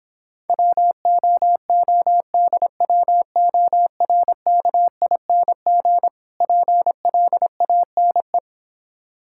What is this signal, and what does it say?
Morse code, heard as a keyed tone, about 26 wpm, 700 Hz. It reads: WOODWORKING PLANE